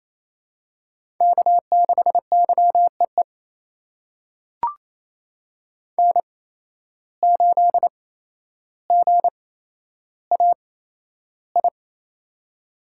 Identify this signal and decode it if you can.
Morse code, heard as a keyed tone, about 28 wpm, spaced as for 12 wpm, 700 Hz. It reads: K6YEE N8GAI
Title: USA Call Signs - 5 Characters - 8x Character Spacing - 28wpm